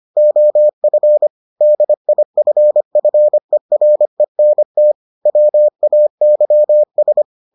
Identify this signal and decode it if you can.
Morse code, heard as a keyed tone, about 25 wpm, 600 Hz. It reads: OF DIFFERENT WAYS